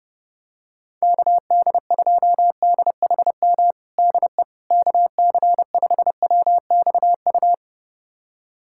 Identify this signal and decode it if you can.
Morse code, heard as a keyed tone, about 30 wpm, 700 Hz. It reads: KD2DHM DE KC5WXU
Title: USA Callsign DE Callsign - 6 Characters - 30wpm